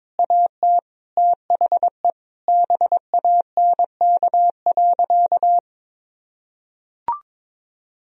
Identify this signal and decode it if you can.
Morse code, heard as a keyed tone, about 22 wpm, 700 Hz. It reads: AT THE BANK.
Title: Sentences from Top 1000 Words - Review Part 4 - 22wpm